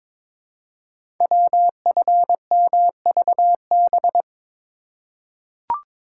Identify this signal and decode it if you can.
Morse code, heard as a keyed tone, about 22 wpm, 700 Hz. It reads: WFMVB
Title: Code Groups - 5 Letters - 22wpm